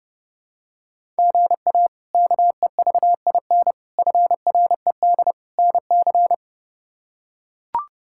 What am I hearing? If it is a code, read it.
Morse code, heard as a keyed tone, about 30 wpm, 700 Hz. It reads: GA KEVIN FRED NC
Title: K1USN SST QSO Practice - 30wpm